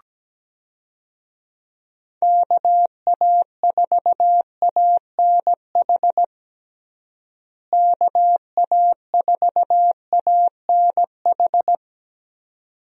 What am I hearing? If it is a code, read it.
Morse code, heard as a keyed tone, about 17 wpm, 700 Hz. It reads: KA4ANH KA4ANH
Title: USA Call Signs - 6 Characters - Repeated 2x - 17wpm